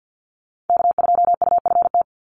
Morse code, heard as a keyed tone, about 50 wpm, 700 Hz. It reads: K2UFT
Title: Call Signs - 50wpm